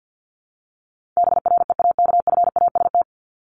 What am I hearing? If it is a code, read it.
Morse code, heard as a keyed tone, about 50 wpm, 700 Hz. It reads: BREAKFAST
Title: Top 1251-1500 Words 50wpm